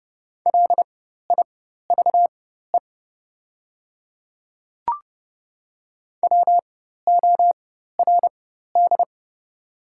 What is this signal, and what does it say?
Morse code, heard as a keyed tone, about 30 wpm, spaced as for 14 wpm, 700 Hz. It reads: LIVE WORD
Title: 4x Character Spacing - Top 200 Words - 30wpm